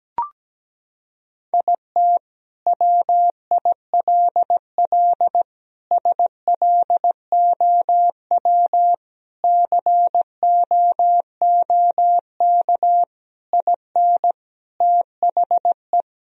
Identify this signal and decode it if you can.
Morse code, heard as a keyed tone, about 17 wpm, 700 Hz. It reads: IT WILL SLOW COOK IN THE
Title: Sentences from Top 1000 Words - Review Part 4 - 17wpm